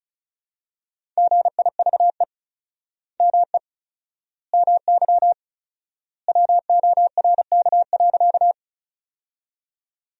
Morse code, heard as a keyed tone, about 36 wpm, 700 Hz. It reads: GIVE ME MY WORK.
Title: 4x Word-Spacing - Sentences from Top 200 Words - 35wpm